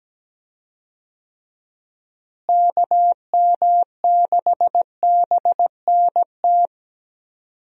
Morse code, heard as a keyed tone, about 17 wpm, 700 Hz. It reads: KM6BNT
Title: USA Call Signs - 6 Characters - Repeated 2x - 17wpm